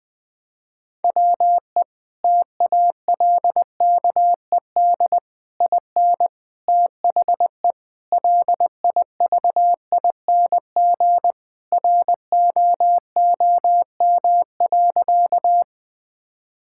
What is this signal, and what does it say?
Morse code, heard as a keyed tone, about 20 wpm, 700 Hz. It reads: WE TALKED IN THE LIVING ROOM.
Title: Sentences with 4-Word Phrases - Spoken Beginning - 20wpm